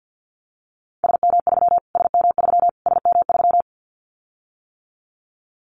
Morse code, heard as a keyed tone, about 50 wpm, 700 Hz. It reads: SM3 SM3 SM3
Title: Callsign Prefixes for Top 25 DXCC - Repeated 3x - 50wpm